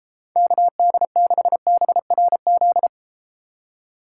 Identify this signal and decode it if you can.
Morse code, heard as a keyed tone, about 32 wpm, 700 Hz. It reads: KD6BRZ